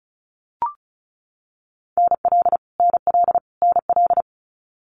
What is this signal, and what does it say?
Morse code, heard as a keyed tone, about 36 wpm, 700 Hz. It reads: NL NL NL
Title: Canadian Provinces and Territories - Repeated 3x - 35wpm